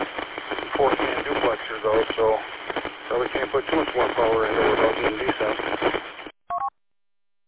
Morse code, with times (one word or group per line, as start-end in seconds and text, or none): none